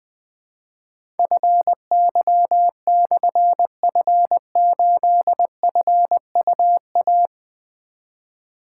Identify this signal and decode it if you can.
Morse code, heard as a keyed tone, about 20 wpm, 700 Hz. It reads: FY/F8FUA
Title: Difficult Call Signs 20wpm